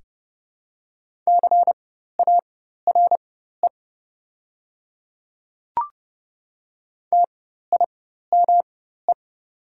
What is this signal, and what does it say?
Morse code, heard as a keyed tone, about 29 wpm, spaced as for 14 wpm, 700 Hz. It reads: CARE TIME
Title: 4x Character Spacing - Top 500 Words - 30wpm